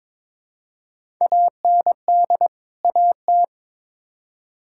Morse code, heard as a keyed tone, about 22 wpm, 700 Hz. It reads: AND AT
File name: practice / 22wpm / Build A Phrase - 5-Word Phrases - 22wpm